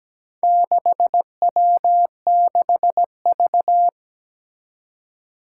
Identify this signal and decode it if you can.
Morse code, heard as a keyed tone, about 17 wpm, 700 Hz. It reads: 6W6V
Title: Four Challenging Letter-Number Combinations 17wpm